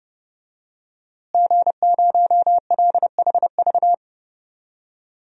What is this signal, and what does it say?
Morse code, heard as a keyed tone, about 30 wpm, 700 Hz. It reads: G0LHV
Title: DXCC-100 Call Signs - 5 Characters - Repeated 3x - 30wpm